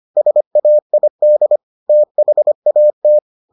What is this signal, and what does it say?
Morse code, heard as a keyed tone, about 25 wpm, 600 Hz. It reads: SAID THAT